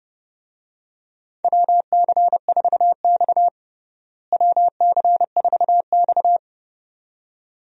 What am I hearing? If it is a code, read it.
Morse code, heard as a keyed tone, about 30 wpm, 700 Hz. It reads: WC4X WC4X
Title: USA Call Signs - 4 Characters - Repeated 2x - 30wpm